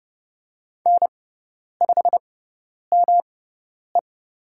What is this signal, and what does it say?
Morse code, heard as a keyed tone, about 30 wpm, spaced as for 10 wpm, 690 Hz. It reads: N5ME
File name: Lesson 18g - Letter m - Call Signs 30wpm